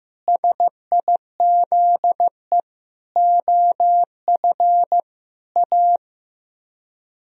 Morse code, heard as a keyed tone, about 15 wpm, 700 Hz. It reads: SIZE OF A
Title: Speed-Racing - Sets of 5 Words 15wpm